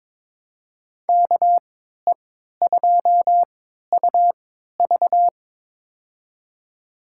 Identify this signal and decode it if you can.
Morse code, heard as a keyed tone, about 22 wpm, spaced as for 13 wpm, 700 Hz. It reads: KE2UV
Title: USA Call Signs - 5 Characters - 3x Character Spacing - 22wpm